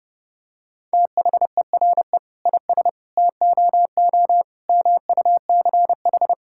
Morse code, heard as a keyed tone, about 30 wpm, 700 Hz. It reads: THERE IS TOO MUCH
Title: Sentences from Top 1000 Words - Review Part 5 - 30wpm